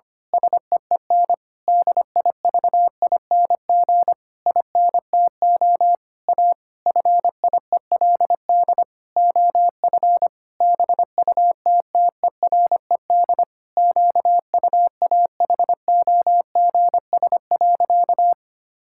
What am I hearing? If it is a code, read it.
Morse code, heard as a keyed tone, about 25 wpm, 700 Hz. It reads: SEEN DIVING INTO A FIELD OF BUTTERED QUAHOGS.
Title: Pangram Challenge v2 - 25wpm